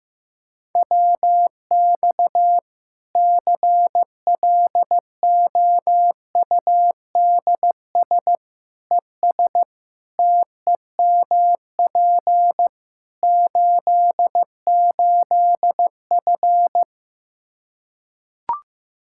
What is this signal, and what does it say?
Morse code, heard as a keyed tone, about 15 wpm, 700 Hz. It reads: WX CLOUDS ES TEMP 88F
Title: K1USN SST QSO Practice 15wpm